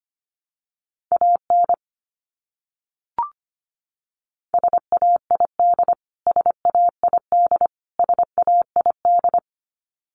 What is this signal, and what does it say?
Morse code, heard as a keyed tone, about 25 wpm, 700 Hz. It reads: AN SAID SAID SAID
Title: Top 500 Words Repeated 25wpm